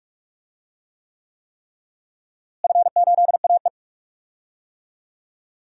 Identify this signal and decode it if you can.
Morse code, heard as a keyed tone, about 42 wpm, 700 Hz. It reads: F8AE
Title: DXCC-100 Call Signs - 4 Characters - Repeated 3x - 45wpm